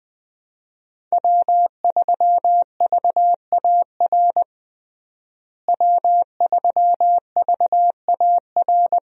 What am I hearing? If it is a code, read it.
Morse code, heard as a keyed tone, about 21 wpm, 700 Hz. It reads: W3VAR W3VAR
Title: USA Call Signs - 5 Characters - Repeated 3x - 20wpm